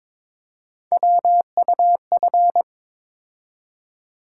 Morse code, heard as a keyed tone, about 22 wpm, 700 Hz. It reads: WUF
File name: practice / 22wpm / Three Letter-Number Combinations - 22wpm